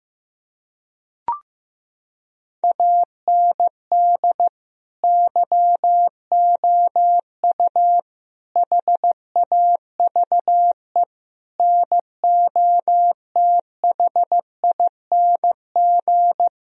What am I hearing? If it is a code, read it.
Morse code, heard as a keyed tone, about 15 wpm, 700 Hz. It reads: AND YOU HAVE NOTHING
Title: Sentences from Top 1000 Words - Review Part 1 - 15wpm